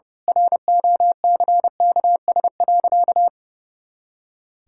Morse code, heard as a keyed tone, about 32 wpm, 700 Hz. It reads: ROCKS.
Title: Pangram Challenge v2 - 30wpm